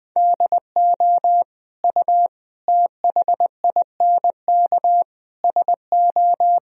Morse code, heard as a keyed tone, about 20 wpm, 700 Hz. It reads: DO U THINK SO
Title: Sentences from Top 200 Words and Abbreviations 20wpm